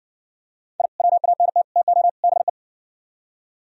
Morse code, heard as a keyed tone, about 60 wpm, 700 Hz. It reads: I WANT TO BE